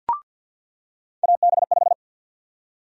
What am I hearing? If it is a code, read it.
Morse code, heard as a keyed tone, about 47 wpm, 700 Hz. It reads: AB5
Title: USA Call Sign Prefixes - 50wpm